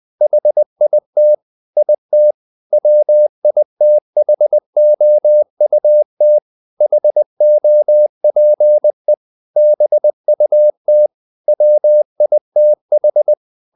Morse code, heard as a keyed tone, about 20 wpm, 600 Hz. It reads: HIT IT WITHOUT HOPE BUT WITH
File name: the old man and the sea part 2 20wpm